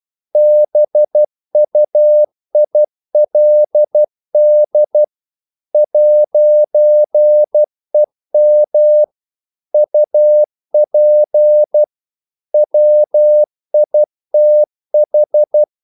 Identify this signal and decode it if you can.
Morse code, heard as a keyed tone, about 12 wpm, 600 Hz. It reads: BUILD 'EM UP WITH
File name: CW / Practice / If12